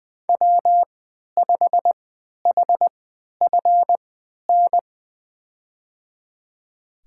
Morse code, heard as a keyed tone, about 20 wpm, spaced as for 11 wpm, 700 Hz. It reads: W5HFN
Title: USA Call Signs - 3x Character Spacing - 20wpm